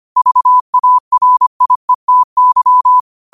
Morse code, heard as a keyed tone, about 25 wpm, 1000 Hz. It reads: VARIETY